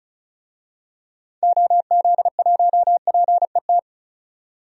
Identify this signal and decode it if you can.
Morse code, heard as a keyed tone, about 34 wpm, 700 Hz. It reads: OZ1PET
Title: DXCC-100 Call Signs - 6 Characters - 35wpm